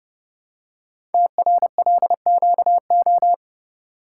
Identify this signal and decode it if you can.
Morse code, heard as a keyed tone, about 30 wpm, 700 Hz. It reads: TRLQO
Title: Code Groups - 5 Letter:Numbers - 30wpm